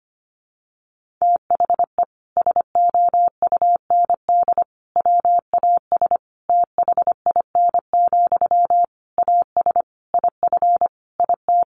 Morse code, heard as a keyed tone, about 25 wpm, 700 Hz. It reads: THE SOUND WAS THIN, AS IF IT
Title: Sentences from Top 1200 Words 25wpm